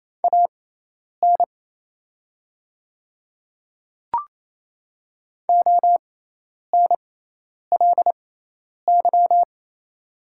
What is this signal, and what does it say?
Morse code, heard as a keyed tone, about 28 wpm, spaced as for 10 wpm, 700 Hz. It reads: AN ONLY
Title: 6x Character Spacing - Top 300 Words - 28wpm